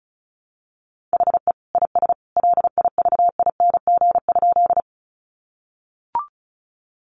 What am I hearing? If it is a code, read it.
Morse code, heard as a keyed tone, about 34 wpm, 700 Hz. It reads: HE IS LIVING?
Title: Sentences from Top 1100 Words 35wpm